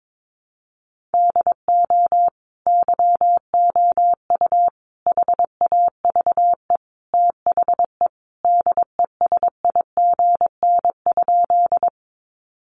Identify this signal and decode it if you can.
Morse code, heard as a keyed tone, about 22 wpm, 700 Hz. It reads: DO YOU HAVE THE DESIGN?
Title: Sentences from Top 800 Words 22wpm